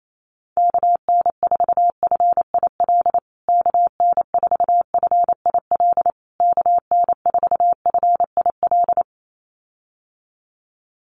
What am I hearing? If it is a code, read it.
Morse code, heard as a keyed tone, about 28 wpm, 700 Hz. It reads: KN4FIL KN4FIL KN4FIL
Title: USA Call Signs Repeated 3x 28wpm